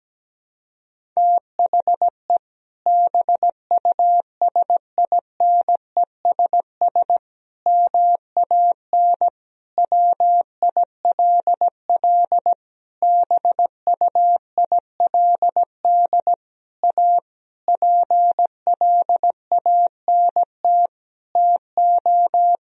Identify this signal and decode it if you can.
Morse code, heard as a keyed tone, about 17 wpm, 700 Hz. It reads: THE BUSINESS MAN WILL BUILD A PLANT TO
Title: Sentences from Top 1000 Words - Review Part 1 - 17wpm